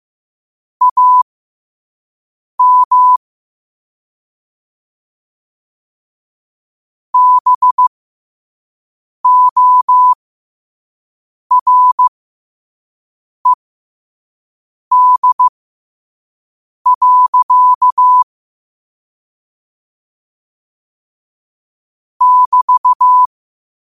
Morse code, heard as a keyed tone, about 15 wpm, spaced as for 5 wpm, 1000 Hz. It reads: AM BORED. <BT>
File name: qotdfiles / 5wpm